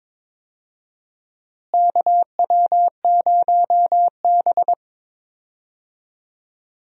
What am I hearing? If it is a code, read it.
Morse code, heard as a keyed tone, about 22 wpm, 700 Hz. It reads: KW0B
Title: USA Call Signs - 4 Characters - Repeated 3x - 22wpm